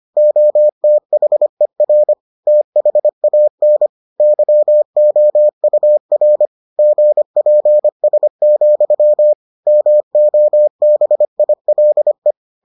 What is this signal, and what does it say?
Morse code, heard as a keyed tone, about 25 wpm, 600 Hz. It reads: OTHER THAN YOUR GPS, MOBILE